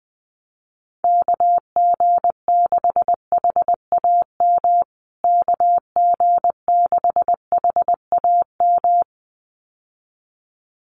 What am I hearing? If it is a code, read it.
Morse code, heard as a keyed tone, about 20 wpm, 700 Hz. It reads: KG6HAM KG6HAM
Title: USA Call Signs Repeated 2x 20wpm